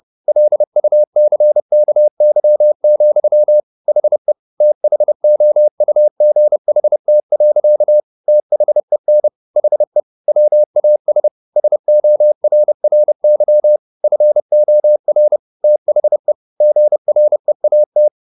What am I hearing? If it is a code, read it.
Morse code, heard as a keyed tone, about 30 wpm, 600 Hz. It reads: LUCKY, HE THOUGHT. THEN HE WAS SORRY FOR THE GREAT